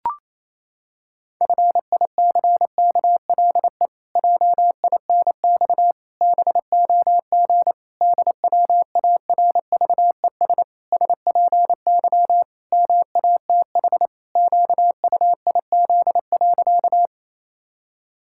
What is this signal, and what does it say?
Morse code, heard as a keyed tone, about 28 wpm, 700 Hz. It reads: FICKLE JINX BOG DWARVES SPY MATH QUIZ.
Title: Pangram Challenge v2 - 28wpm